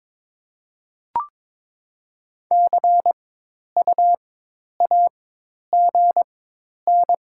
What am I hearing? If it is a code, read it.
Morse code, heard as a keyed tone, about 22 wpm, spaced as for 10 wpm, 700 Hz. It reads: CUAGN
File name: Top 20 QSO Elements - 4x Character Spacing - 22wpm